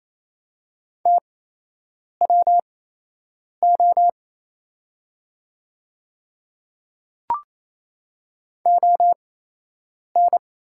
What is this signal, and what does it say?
Morse code, heard as a keyed tone, about 27 wpm, spaced as for 8 wpm, 700 Hz. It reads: TWO ON
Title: 8x Character Spacing - Top 300 Words - 28wpm